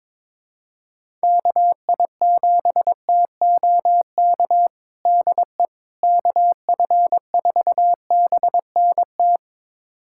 Morse code, heard as a keyed tone, about 22 wpm, 700 Hz. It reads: KI7TOK DE KF4BNT
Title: USA Callsign DE Callsign - 6 Characters - 22wpm